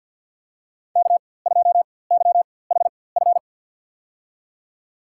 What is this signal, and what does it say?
Morse code, heard as a keyed tone, about 53 wpm, spaced as for 24 wpm, 700 Hz. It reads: K2YHF
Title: USA Call Signs - 5 Characters - 4x Character Spacing - 50wpm